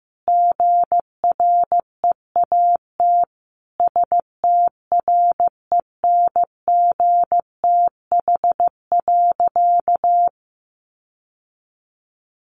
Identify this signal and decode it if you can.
Morse code, heard as a keyed tone, about 15 wpm, 700 Hz. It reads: GREAT STRENGTH.
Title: Sentences from Top 1300 Words 15wpm